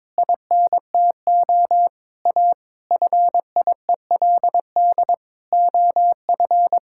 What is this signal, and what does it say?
Morse code, heard as a keyed tone, about 22 wpm, 700 Hz. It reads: INTO A FIELD OF
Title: Pangram Challenge v2 - 22wpm